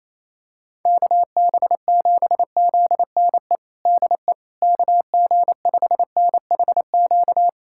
Morse code, heard as a keyed tone, about 28 wpm, 700 Hz. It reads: KB7ZNE DE KG5NHQ
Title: USA Callsign DE Callsign 28wpm